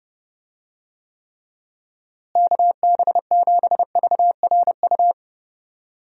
Morse code, heard as a keyed tone, about 32 wpm, 700 Hz. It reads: KB7VRU
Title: USA Call Signs - 6 Characters - Repeated 2x - 30wpm